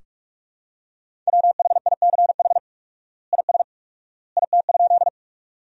Morse code, heard as a keyed tone, about 46 wpm, 700 Hz. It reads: WHICH IS IT?
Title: 4x Word-Spacing - Sentences from Top 100 Words - 45wpm